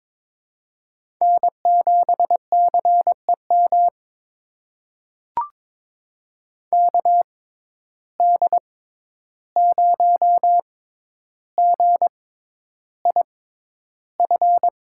Morse code, heard as a keyed tone, about 22 wpm, spaced as for 11 wpm, 700 Hz. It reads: N7CEM KD0GIF
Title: USA Call Signs - 6x Character Spacing - 22wpm